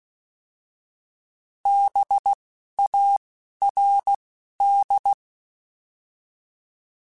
Morse code, heard as a keyed tone, about 16 wpm, 800 Hz. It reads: B A R D